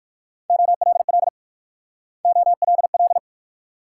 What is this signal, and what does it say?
Morse code, heard as a keyed tone, about 53 wpm, 690 Hz. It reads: OLL OLL
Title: Speed-Racing Multi-letter Phonemes - 45wpm